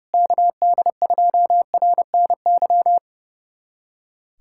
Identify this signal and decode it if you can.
Morse code, heard as a keyed tone, about 30 wpm, 700 Hz. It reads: KD2RNY
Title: USA Callsign DE Callsign 30wpm